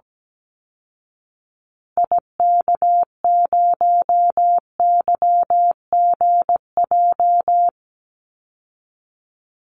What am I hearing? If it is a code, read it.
Morse code, heard as a keyed tone, about 17 wpm, 700 Hz. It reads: IK0YGJ
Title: Call Signs Repeated 2x - 17wpm